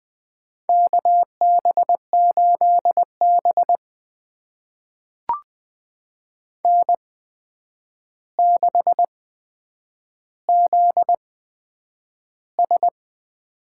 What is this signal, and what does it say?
Morse code, heard as a keyed tone, about 20 wpm, spaced as for 9 wpm, 700 Hz. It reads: KB8B N6ZS
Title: USA Call Signs - 4 Characters - 8x Character Spacing - 20wpm